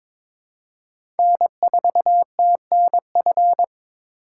Morse code, heard as a keyed tone, about 22 wpm, 700 Hz. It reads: N4TNF
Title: USA Call Signs - 5 Characters - 4x Character Spacing - 22wpm